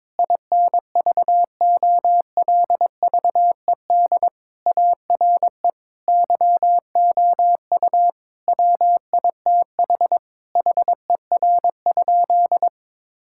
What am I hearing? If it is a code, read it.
Morse code, heard as a keyed tone, about 21 wpm, 700 Hz. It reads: INVOLVED ARE YOU WITH HER?
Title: Sentences from Top 1000 Words - Review Part 5 - 22wpm